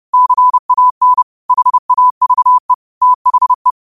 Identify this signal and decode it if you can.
Morse code, heard as a keyed tone, about 30 wpm, 1000 Hz. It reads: CAN HAVE THE
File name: qotdfiles / 30wpm